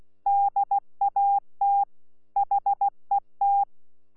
Morse code, heard as a keyed tone, about 16 wpm, 800 Hz. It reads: DAT HET
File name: Z PI4VRZ / PI4VRZ 14-jun-2025